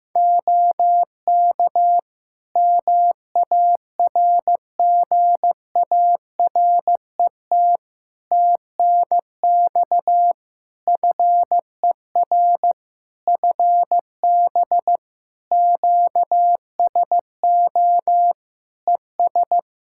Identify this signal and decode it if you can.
Morse code, heard as a keyed tone, about 15 wpm, 700 Hz. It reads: OK MARGARET TNX FER FB QSO ES
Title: Practice QSO - Exchange 4 - Ending - 15wpm